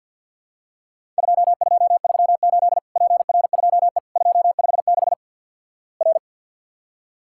Morse code, heard as a keyed tone, about 48 wpm, 700 Hz. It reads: 2238 PR2E 256 R
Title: Worked All Europe DX Contest - QTC Exchange - 50wpm